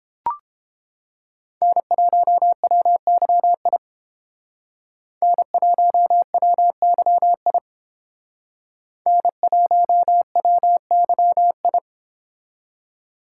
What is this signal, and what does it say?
Morse code, heard as a keyed tone, about 31 wpm, 700 Hz. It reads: N1WYI N1WYI N1WYI